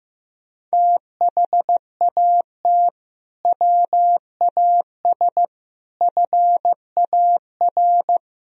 Morse code, heard as a keyed tone, about 15 wpm, 700 Hz. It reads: THAT WAS FAR
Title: Sentences from Top 1000 Words - Review Part 3 - 15wpm